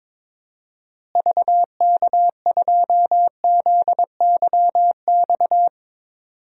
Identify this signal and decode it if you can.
Morse code, heard as a keyed tone, about 22 wpm, 700 Hz. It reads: VK2ZYX
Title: DXCC-100 Call Signs - 6 Characters - Repeated 3x - 22wpm